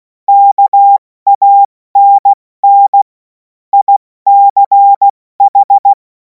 Morse code, heard as a keyed tone, about 16 wpm, 800 Hz. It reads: KANN ICH